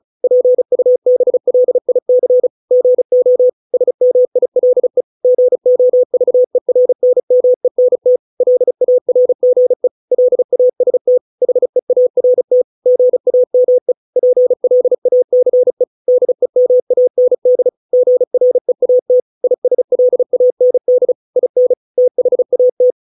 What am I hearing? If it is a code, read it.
Morse code, heard as a keyed tone, about 34 wpm, 500 Hz. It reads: PUBLIC GO SMILE GOVERNMENT LARGE LAST HEART GAME PLACE DEMAND GREAT ISLAND IN THAT